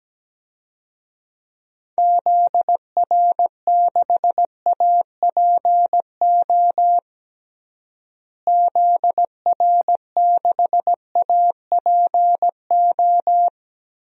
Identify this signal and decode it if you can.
Morse code, heard as a keyed tone, about 17 wpm, 700 Hz. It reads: ZR6APO ZR6APO
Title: DXCC-100 Call Signs - 6 Characters - Repeated 3x - 17wpm